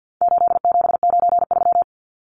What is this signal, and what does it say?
Morse code, heard as a keyed tone, about 50 wpm, 700 Hz. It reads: 8793
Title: Numbers - 4-Digit 50wpm